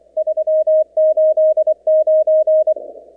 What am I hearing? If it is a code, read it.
Morse code, heard as a keyed tone, about 25 wpm, 630 Hz. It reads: 389
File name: contests / 2019-cw-wpx-kt5j-k5pi- -2330 0-29